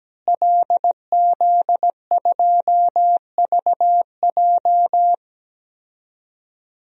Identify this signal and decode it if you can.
Morse code, heard as a keyed tone, about 17 wpm, 700 Hz. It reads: LZ2VJ
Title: DXCC-100 Call Signs - 5 Characters - Repeated 2x - 17wpm